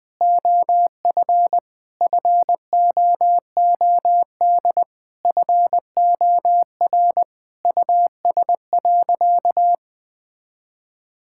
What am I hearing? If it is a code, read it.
Morse code, heard as a keyed tone, about 20 wpm, 700 Hz. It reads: OF FOOD FOR US.
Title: Sentences from Top 1000 Words - Review Part 5 - 20wpm